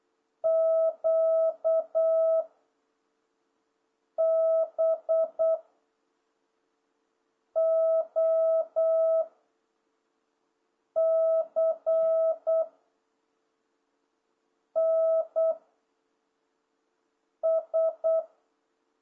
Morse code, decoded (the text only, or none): Q B O C N S